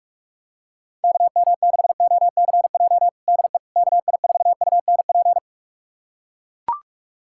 Morse code, heard as a keyed tone, about 41 wpm, 700 Hz. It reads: KM6OCJ DE KI4UNP
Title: USA Callsign DE Callsign - 6 Characters - 45wpm